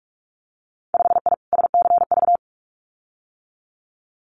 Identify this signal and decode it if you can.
Morse code, heard as a keyed tone, about 44 wpm, 700 Hz. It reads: 5I SCV
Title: Winter Field Day Exchanges Encore 45wpm